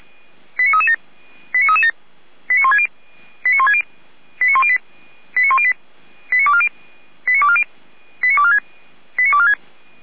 Five-tone selective call (ZVEI), heard as quick runs of five tones, tones of about 70 ms. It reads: 89337 89337 89160 89160 89108 89108 89240 89240 89246 89246
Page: SDR/ZVEI